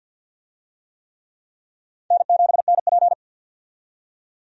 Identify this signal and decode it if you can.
Morse code, heard as a keyed tone, about 51 wpm, 700 Hz. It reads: N7NP
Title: USA Call Signs - 4 Characters - 50wpm